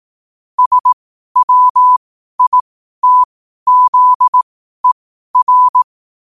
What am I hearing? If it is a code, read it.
Morse code, heard as a keyed tone, about 18 wpm, spaced as for 12 wpm, 1000 Hz. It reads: SWITZER
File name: qotdfiles / 12wpm